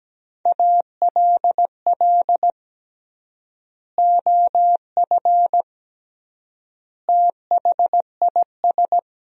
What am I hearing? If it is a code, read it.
Morse code, heard as a keyed tone, about 17 wpm, 700 Hz. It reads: ALL OF THIS
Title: 3x Word-Spacing - Sentences from Top 500 Words - 17wpm